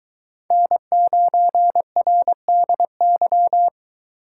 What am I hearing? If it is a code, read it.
Morse code, heard as a keyed tone, about 22 wpm, 700 Hz. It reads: N9RDY